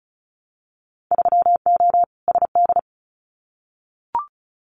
Morse code, heard as a keyed tone, about 36 wpm, 690 Hz. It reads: 3O SD
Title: Winter Field Day Exchanges 35wpm